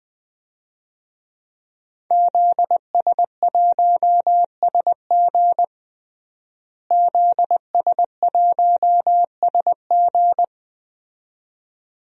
Morse code, heard as a keyed tone, about 20 wpm, 700 Hz. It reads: ZS1SG ZS1SG
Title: DXCC-100 Call Signs - 5 Characters - Repeated 2x - 20wpm